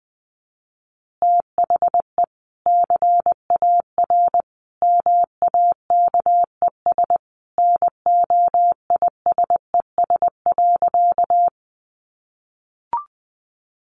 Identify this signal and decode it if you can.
Morse code, heard as a keyed tone, about 20 wpm, 700 Hz. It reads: THE CAR MAKES NOISES.
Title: Easy Phrases 20wpm